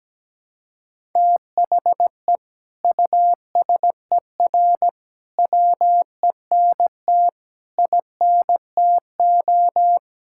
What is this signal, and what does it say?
Morse code, heard as a keyed tone, about 17 wpm, 700 Hz. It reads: THE USER WENT INTO
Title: Sentences from Top 1000 Words - Review Part 5 - 17wpm